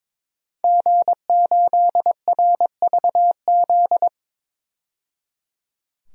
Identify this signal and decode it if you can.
Morse code, heard as a keyed tone, about 22 wpm, 700 Hz. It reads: G8RVZ
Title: DXCC-100 Call Signs - 5 Characters - 22wpm